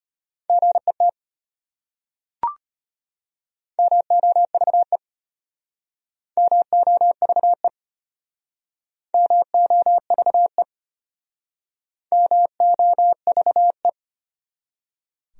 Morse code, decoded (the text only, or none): GET MOVE MOVE MOVE MOVE